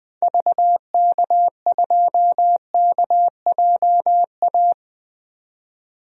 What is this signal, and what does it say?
Morse code, heard as a keyed tone, about 20 wpm, 700 Hz. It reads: VK2KJA
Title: DXCC-100 Call Signs - 6 Characters - 20wpm